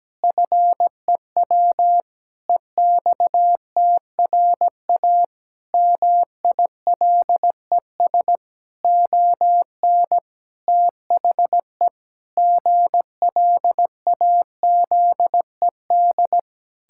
Morse code, heard as a keyed tone, about 17 wpm, 700 Hz. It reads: FEW EXTRA MILES ON THE GLAZED
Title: Pangram Challenge v2 - 17wpm